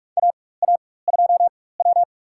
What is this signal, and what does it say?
Morse code, heard as a keyed tone, about 42 wpm, spaced as for 21 wpm, 690 Hz. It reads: AA2W
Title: USA Call Signs - 4x Character Spacing - 45wpm